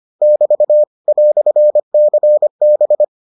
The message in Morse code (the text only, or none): <BT> "CB